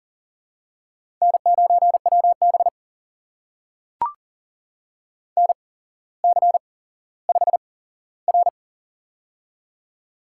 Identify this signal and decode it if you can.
Morse code, heard as a keyed tone, about 39 wpm, spaced as for 17 wpm, 700 Hz. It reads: N9WB NC5R
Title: USA Call Signs - 4 Characters - 8x Character Spacing - 40wpm